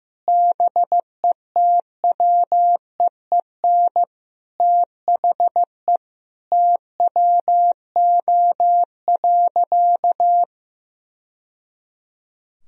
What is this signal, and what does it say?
Morse code, heard as a keyed tone, about 15 wpm, 700 Hz. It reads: BETWEEN THE TWO.
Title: Sentences from Top 1000 Words - Review Part 4 - 15wpm